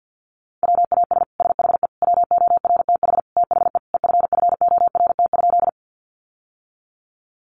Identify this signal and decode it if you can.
Morse code, heard as a keyed tone, about 50 wpm, 700 Hz. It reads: WAS SHE WORTH THE EFFORT?
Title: Sentences from Top 1200 Words 50wpm